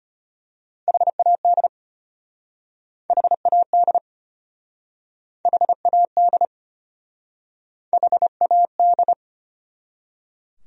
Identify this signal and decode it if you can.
Morse code, heard as a keyed tone, about 33 wpm, 700 Hz. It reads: HAD HAD HAD HAD